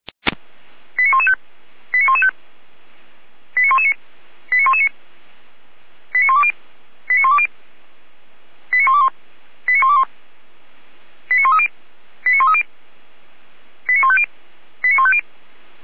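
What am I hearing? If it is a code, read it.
Five-tone selective call (ZVEI), heard as quick runs of five tones, tones of about 70 ms. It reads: 89115 89115 89119 89119 89120 89120 89121 89121 89130 89130 89150 89150
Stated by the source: ZVEI five-tone selective call 89110 received by SDR.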